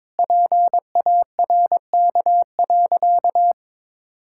Morse code, heard as a keyed tone, about 22 wpm, 700 Hz. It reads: PARK.
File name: practice / 22wpm / Sentences with 2-Word Phrases - 22wpm